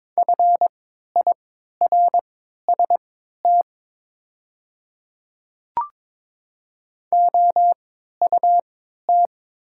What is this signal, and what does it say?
Morse code, heard as a keyed tone, about 23 wpm, spaced as for 13 wpm, 700 Hz. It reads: FIRST OUT
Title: 3x Character Spacing - Top 200 Words - 22wpm